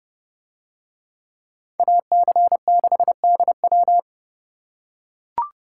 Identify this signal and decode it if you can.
Morse code, heard as a keyed tone, about 30 wpm, 700 Hz. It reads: AC6DW